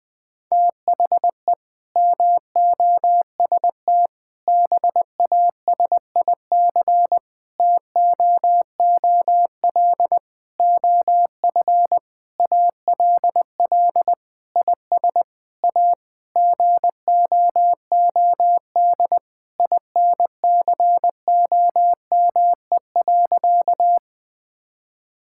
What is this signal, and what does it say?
Morse code, heard as a keyed tone, about 20 wpm, 700 Hz. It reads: THE MOST BASIC TOOL OF ALL IS A GOOD INCOME.
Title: Sentences from Top 1000 Words - Review Part 6 - 20wpm